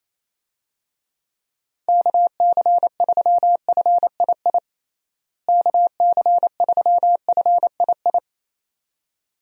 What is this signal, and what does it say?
Morse code, heard as a keyed tone, about 28 wpm, 700 Hz. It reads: KC3FII KC3FII
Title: USA Call Signs - 6 Characters - Repeated 2x - 28wpm